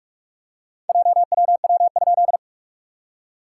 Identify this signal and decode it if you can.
Morse code, heard as a keyed tone, about 44 wpm, 690 Hz. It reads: JWW?
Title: SOTA QSO Practice 45wpm